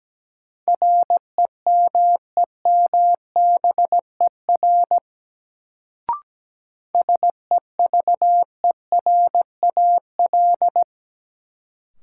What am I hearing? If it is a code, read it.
Morse code, heard as a keyed tone, about 17 wpm, 700 Hz. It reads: REMEMBER SEVERAL
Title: Sentences from Top 400 Words - New Word Drill - 17wpm